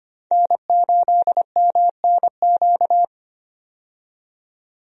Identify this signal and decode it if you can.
Morse code, heard as a keyed tone, about 25 wpm, 700 Hz. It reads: N8MNQ
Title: USA Call Signs - 5 Characters - 25wpm